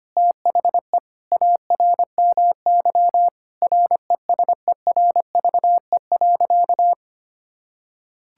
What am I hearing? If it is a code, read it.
Morse code, heard as a keyed tone, about 25 wpm, 700 Hz. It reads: THE ARMY RESERVE.
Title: Sentences from Top 1000 Words - Review Part 6 - 25wpm